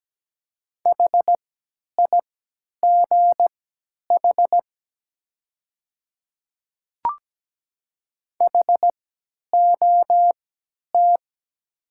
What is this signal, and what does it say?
Morse code, heard as a keyed tone, about 17 wpm, spaced as for 10 wpm, 700 Hz. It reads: HIGH HOT
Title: 3x Character Spacing - Top 200 Words - 17wpm